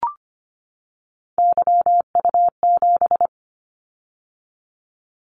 Morse code, heard as a keyed tone, about 25 wpm, 700 Hz. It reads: YU7
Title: Callsign Prefixes for Top 25 DXCC - 25wpm